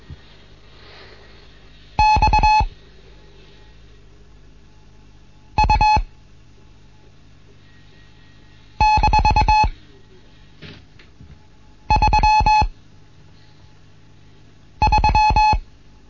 Morse code, heard as a keyed tone, about 22 wpm, spaced as for 3 wpm, 860 Hz. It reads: XU-33
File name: Morsen / Kurs A 15